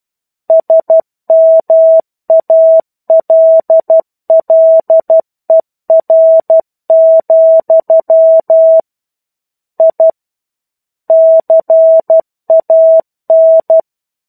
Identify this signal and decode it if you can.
Morse code, heard as a keyed tone, about 12 wpm, 650 Hz. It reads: SMALLER, I CAN
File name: 01-Chapter-01 12 WPM